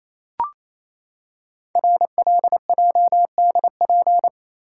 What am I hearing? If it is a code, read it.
Morse code, heard as a keyed tone, about 29 wpm, 700 Hz. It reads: RLJDP
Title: Code Groups - 5 Letters - 28wpm